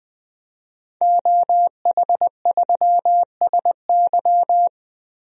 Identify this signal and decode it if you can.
Morse code, heard as a keyed tone, about 20 wpm, 700 Hz. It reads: OH3SY
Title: DXCC-100 Call Signs - 5 Characters - Repeated 2x - 20wpm